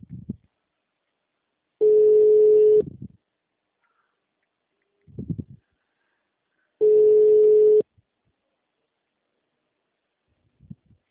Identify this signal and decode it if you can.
Ringback tone (Europe, 425 Hz).
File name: out- -2011- - - .27